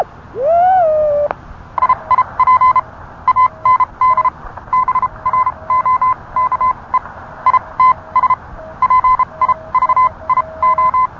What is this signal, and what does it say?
Morse code, heard as a keyed tone, about 30 wpm, spaced as for 24 wpm, 1030 Hz. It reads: IIP AND BROKE ITS PIVIO